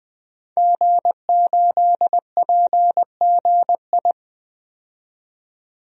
Morse code, heard as a keyed tone, about 20 wpm, 700 Hz. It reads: G8PGI
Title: DXCC-100 Call Signs - 5 Characters - 20wpm